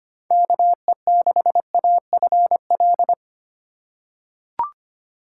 Morse code, heard as a keyed tone, about 25 wpm, 700 Hz. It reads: KE6AFL
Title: USA Call Signs - 6 Characters - 4x Character Spacing - 25wpm